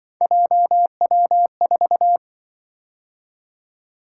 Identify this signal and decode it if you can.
Morse code, heard as a keyed tone, about 25 wpm, 700 Hz. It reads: JW4
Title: Speed-Racing - Three Letter-Number Combinations - 20wpm